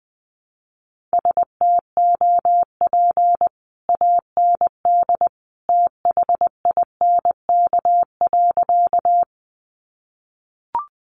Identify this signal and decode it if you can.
Morse code, heard as a keyed tone, about 20 wpm, 700 Hz. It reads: STOP AND THINK.
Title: Sentences from Top 300 Words 20wpm